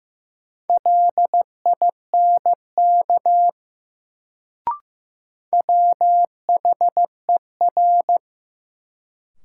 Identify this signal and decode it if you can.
Morse code, heard as a keyed tone, about 15 wpm, 700 Hz. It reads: LINK WHER
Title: Sentences from Top 1500 Words - New Word Drill Alt - 15wpm